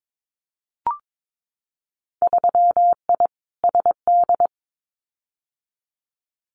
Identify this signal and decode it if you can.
Morse code, heard as a keyed tone, about 22 wpm, 700 Hz. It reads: 3I SD
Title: Winter Field Day Exchanges Encore 22wpm